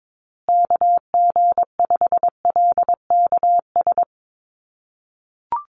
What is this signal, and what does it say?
Morse code, heard as a keyed tone, about 22 wpm, 700 Hz. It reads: KG5LKS
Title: USA Call Signs Repeated 3x 22wpm